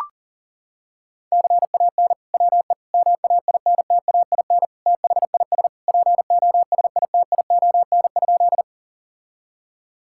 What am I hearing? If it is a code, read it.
Morse code, heard as a keyed tone, about 37 wpm, 700 Hz. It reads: CAN WE MAINTAIN THIS POSITION?
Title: Sentences from Top 1000 Words - Review Part 5 - 40wpm